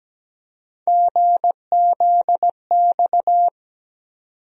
Morse code, heard as a keyed tone, about 17 wpm, 700 Hz. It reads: GZX
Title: USA Call Sign Postfixes - 3 Letters 17wpm